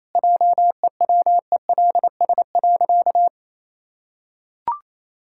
Morse code, heard as a keyed tone, about 28 wpm, 700 Hz. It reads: JEWELS.
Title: Pangram Challenge v2 - 28wpm